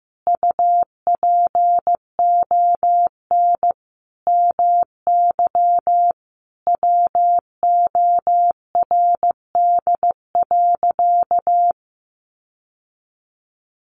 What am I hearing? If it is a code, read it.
Morse code, heard as a keyed tone, about 14 wpm, 700 Hz. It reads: UPON MY WORD.
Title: Sentences from Top 1100 Words 15wpm